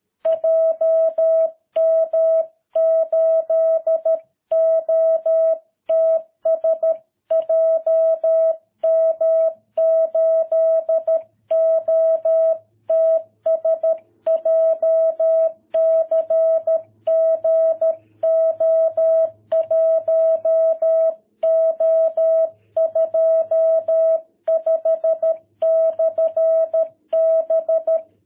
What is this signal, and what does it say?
Morse code, heard as a keyed tone, about 13 wpm, 660 Hz. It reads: JM8OTS JM8OTS JCG O1O25/B